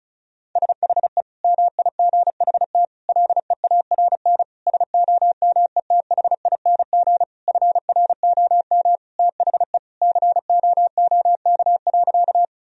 Morse code, heard as a keyed tone, about 34 wpm, 700 Hz. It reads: SHE MIGHT LEARN SOMETHING FROM THE COOK.
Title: Sentences from Top 1000 Words - Review Part 4 - 35wpm